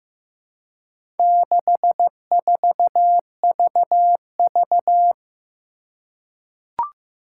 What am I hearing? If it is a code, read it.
Morse code, heard as a keyed tone, about 15 wpm, 700 Hz. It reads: 64VV
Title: Four Challenging Letter-Number Combinations 15wpm